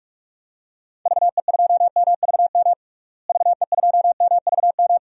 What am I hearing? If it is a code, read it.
Morse code, heard as a keyed tone, about 46 wpm, 700 Hz. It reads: VE2MVM VE2MVM
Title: DXCC-100 Call Signs - 6 Characters - Repeated 2x - 45wpm